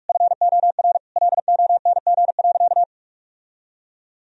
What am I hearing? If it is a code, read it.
Morse code, heard as a keyed tone, about 41 wpm, 700 Hz. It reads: FOR LONG.
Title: Sentences from Top 1000 Words - Review Part 2 - 45wpm